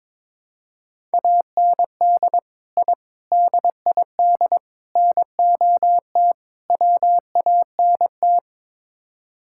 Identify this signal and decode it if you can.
Morse code, heard as a keyed tone, about 22 wpm, 700 Hz. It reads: AND I DID NOT WANT
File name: practice / 22wpm / Build A Phrase - 5-Word Phrases - 22wpm